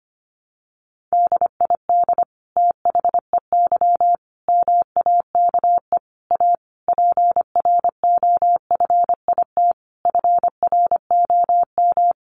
Morse code, heard as a keyed tone, about 25 wpm, 700 Hz. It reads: DID THEY MAKE A PROFIT FROM
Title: Sentences from Top 800 Words 25wpm